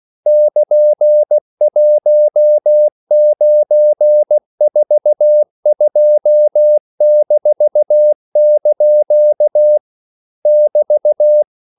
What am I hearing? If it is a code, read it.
Morse code, heard as a keyed tone, about 16 wpm, 600 Hz. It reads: <KN>1942-) <BT>
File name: CW / Practice / EnglishQuotations0216